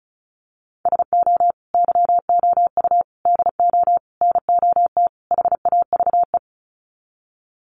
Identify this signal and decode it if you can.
Morse code, heard as a keyed tone, about 34 wpm, 700 Hz. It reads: SO YOU DO NOT HAVE
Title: Sets of 5 Words Encore 35wpm